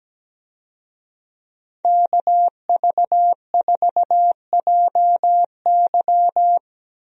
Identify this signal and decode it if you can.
Morse code, heard as a keyed tone, about 17 wpm, 700 Hz. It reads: KV4JY
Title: USA Call Signs - 5 Characters - Repeated 3x - 17wpm